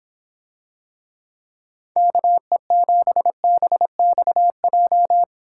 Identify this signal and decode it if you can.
Morse code, heard as a keyed tone, about 26 wpm, 700 Hz. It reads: KE7BXJ